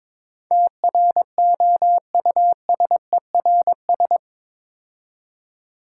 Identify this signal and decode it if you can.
Morse code, heard as a keyed tone, about 22 wpm, 690 Hz. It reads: TROUSERS
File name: Difficult Words with Hints - 22wpm